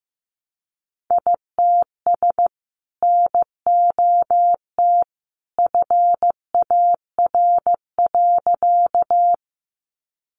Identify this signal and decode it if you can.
Morse code, heard as a keyed tone, about 15 wpm, 700 Hz. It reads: ITS NOT FAR.
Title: Sentences from Top 600 Words 15wpm